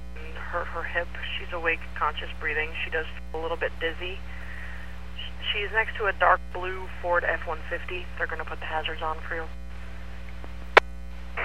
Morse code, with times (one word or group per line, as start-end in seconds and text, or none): none